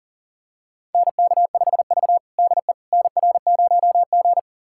Morse code, heard as a keyed tone, about 40 wpm, 700 Hz. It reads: NK5V DE NR0G
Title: USA Callsign DE Callsign - 4 Characters - 40wpm